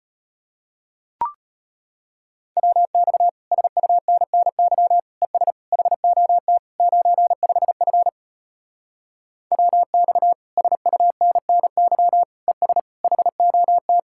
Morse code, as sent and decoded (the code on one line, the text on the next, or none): .-- -..- / ... ..- -. -. -.-- / . ... / .... --- - / ----. ..... ..-. / .-- -..- / ... ..- -. -. -.-- / . ... / .... --- -
WX SUNNY ES HOT 95F WX SUNNY ES HOT